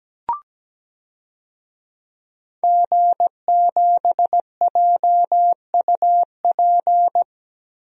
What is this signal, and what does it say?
Morse code, heard as a keyed tone, about 17 wpm, 700 Hz. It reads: G7JUP
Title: DXCC-100 Call Signs - 5 Characters - Repeated 2x - 17wpm